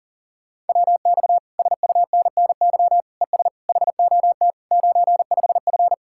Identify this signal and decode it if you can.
Morse code, heard as a keyed tone, about 50 wpm, 700 Hz. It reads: WX SUNNY ES HOT 95F